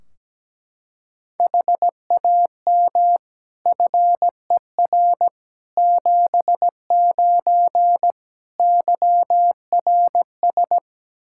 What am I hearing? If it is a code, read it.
Morse code, heard as a keyed tone, about 17 wpm, 700 Hz. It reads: HAM FER 79 YRS
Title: Practice QSO - Exchange 3 - More Info - 17wpm